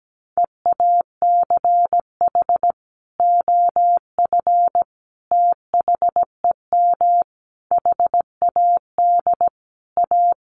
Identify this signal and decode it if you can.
Morse code, heard as a keyed tone, about 17 wpm, 700 Hz. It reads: EACH OF THEM HAD A
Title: Sentences from Top 1300 Words 17wpm